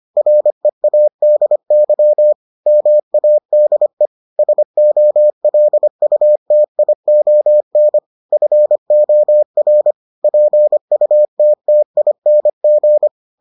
Morse code, heard as a keyed tone, about 25 wpm, 600 Hz. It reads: READY MADE SOLUTION FOR PUTTING